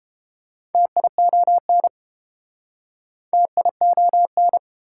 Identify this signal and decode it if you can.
Morse code, heard as a keyed tone, about 29 wpm, 700 Hz. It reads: TION TION